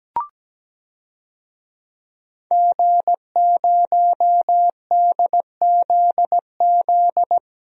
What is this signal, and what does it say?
Morse code, heard as a keyed tone, about 17 wpm, 700 Hz. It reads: G0DZZ
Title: DXCC-100 Call Signs - 5 Characters - Repeated 3x - 17wpm